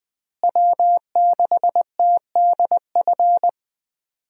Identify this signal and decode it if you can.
Morse code, heard as a keyed tone, about 21 wpm, 700 Hz. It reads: W6TDF
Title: USA Call Signs - 5 Characters - Repeated 3x - 20wpm